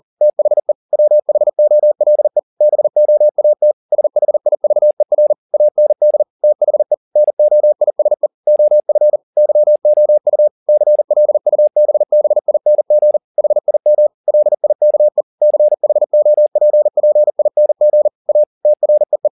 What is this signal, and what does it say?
Morse code, heard as a keyed tone, about 40 wpm, 600 Hz. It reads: THE WHOLE BOAT SHIVER AND THE NOISE OF YOU CLUBBING HIM LIKE CHOPPING A TREE